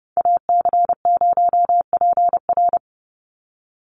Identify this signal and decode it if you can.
Morse code, heard as a keyed tone, about 30 wpm, 700 Hz. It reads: AC0PR
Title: NA SOTA Chaser Callsigns 30wpm